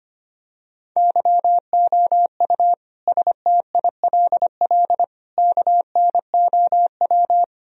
Morse code, heard as a keyed tone, about 25 wpm, 700 Hz. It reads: YOU STILL KNOW
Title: Sentences from Top 1000 Words - Review Part 1 - 25wpm